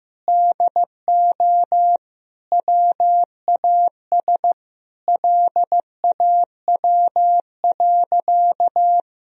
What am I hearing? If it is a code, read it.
Morse code, heard as a keyed tone, about 15 wpm, 700 Hz. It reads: DO WAS LAW.
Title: Sentences from Top 1000 Words - Review Part 3 - 15wpm